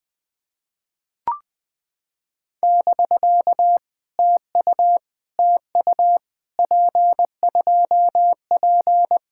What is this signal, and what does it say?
Morse code, heard as a keyed tone, about 19 wpm, 700 Hz. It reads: <BK> TU TU P2P
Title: POTA QSO Practice - 20wpm